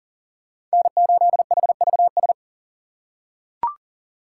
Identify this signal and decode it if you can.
Morse code, heard as a keyed tone, about 40 wpm, 700 Hz. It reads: N8HVS
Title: USA Call Signs - 5 Characters - 2x Character Spacing - 40wpm